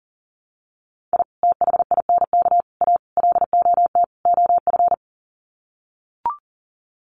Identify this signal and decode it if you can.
Morse code, heard as a keyed tone, about 40 wpm, 700 Hz. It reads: I THINK A LOT OF
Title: Sets of 5 Words 40wpm